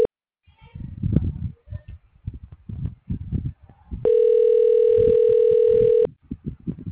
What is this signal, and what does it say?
Ringback tone (North America), 440 + 480 Hz.